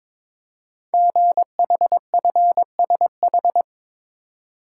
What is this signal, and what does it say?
Morse code, heard as a keyed tone, about 22 wpm, 700 Hz. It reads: GHFSH
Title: Code Groups - 5 Letters - 22wpm